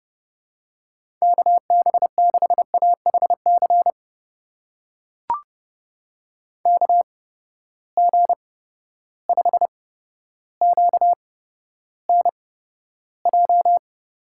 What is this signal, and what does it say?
Morse code, heard as a keyed tone, about 30 wpm, spaced as for 13 wpm, 700 Hz. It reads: KB6AHC KG5QNJ
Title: USA Call Signs - 6 Characters - 8x Character Spacing - 30wpm